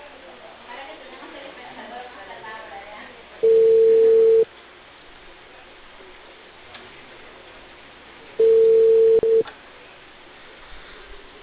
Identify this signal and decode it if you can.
Ringback tone (China, 450 Hz), with speech.